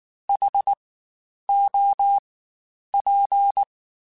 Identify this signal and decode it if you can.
Morse code, heard as a keyed tone, about 19 wpm, spaced as for 9 wpm, 800 Hz. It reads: HOP
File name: Spaced Alphas / Spaced Alphas mp3s / Hs sp @18